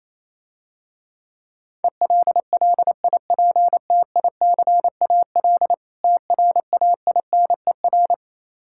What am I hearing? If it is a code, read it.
Morse code, heard as a keyed tone, about 27 wpm, 700 Hz. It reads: ELLIPTICAL TRAINER
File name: Easy Words - Speed-Racing with Hints - 22wpm